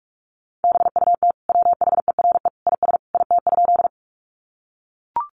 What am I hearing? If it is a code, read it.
Morse code, heard as a keyed tone, about 44 wpm, 700 Hz. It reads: BUT WHERE IS IT?
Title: Sentences from Top 200 Words 45wpm